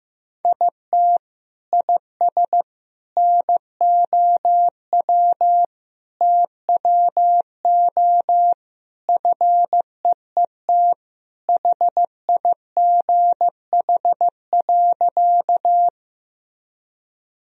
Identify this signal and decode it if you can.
Morse code, heard as a keyed tone, about 15 wpm, 700 Hz. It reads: IT IS NOW TWO FEET HIGH.
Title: Sentences from Top 1000 Words - Review Part 1 - 15wpm